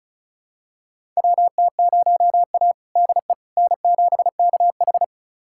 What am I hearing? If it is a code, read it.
Morse code, heard as a keyed tone, about 36 wpm, 700 Hz. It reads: WT0A DE N7KH
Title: USA Callsign DE Callsign - 4 Characters - 35wpm